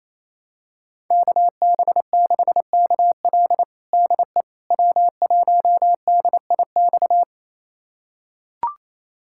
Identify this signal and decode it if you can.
Morse code, heard as a keyed tone, about 28 wpm, 700 Hz. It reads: KB6KL DE W1DIX
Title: USA Callsign DE Callsign 28wpm